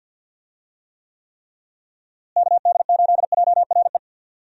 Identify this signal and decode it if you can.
Morse code, heard as a keyed tone, about 50 wpm, 700 Hz. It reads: KD8JRE
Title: USA Call Signs - 6 Characters - Repeated 3x - 50wpm